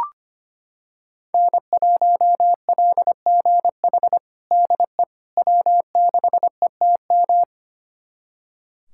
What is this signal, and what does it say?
Morse code, heard as a keyed tone, about 25 wpm, 700 Hz. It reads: N1LGH DE W6ETM
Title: USA Callsign DE Callsign - 5 Characters - 25wpm